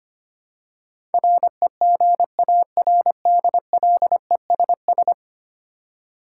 Morse code, heard as a keyed tone, about 25 wpm, 700 Hz. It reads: REGARDLESS
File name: Ten Letter Words - 25wpm